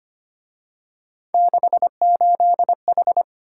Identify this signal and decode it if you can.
Morse code, heard as a keyed tone, about 30 wpm, 700 Hz. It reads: 68H
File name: Speed-Racing - Three Letter-Number Combinations - 25wpm